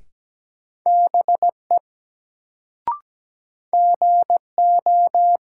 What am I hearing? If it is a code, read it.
Morse code, heard as a keyed tone, about 17 wpm, 700 Hz. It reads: BE GO
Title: Sentences from Top 100 Words - New Word Drill - 17wpm